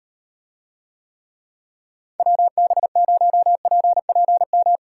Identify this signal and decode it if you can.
Morse code, heard as a keyed tone, about 38 wpm, 700 Hz. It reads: WB0PPM